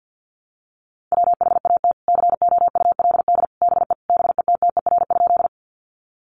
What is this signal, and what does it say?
Morse code, heard as a keyed tone, about 50 wpm, 700 Hz. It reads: WHAT COULD BE BETTER?
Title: Sentences from Top 400 Words 50wpm